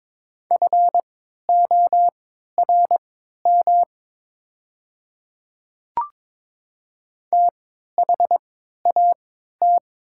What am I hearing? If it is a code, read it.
Morse code, heard as a keyed tone, about 22 wpm, spaced as for 13 wpm, 700 Hz. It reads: FORM THAT
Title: 3x Character Spacing - Top 300 Words - 22wpm